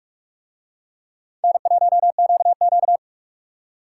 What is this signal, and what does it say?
Morse code, heard as a keyed tone, about 45 wpm, 700 Hz. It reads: N1QQ